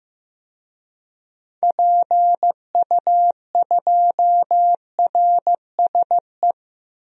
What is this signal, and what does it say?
Morse code, heard as a keyed tone, about 15 wpm, 700 Hz. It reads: PU2RSE
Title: DXCC-100 Call Signs - 6 Characters - 15wpm